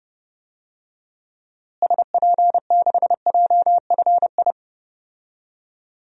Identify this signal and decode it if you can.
Morse code, heard as a keyed tone, about 30 wpm, 700 Hz. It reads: SP6JFI
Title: DXCC-100 Call Signs - 6 Characters - Repeated 2x - 30wpm